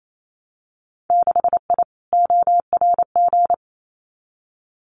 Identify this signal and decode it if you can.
Morse code, heard as a keyed tone, about 28 wpm, 700 Hz. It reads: 6I ORG
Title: Winter Field Day Exchanges Encore 28wpm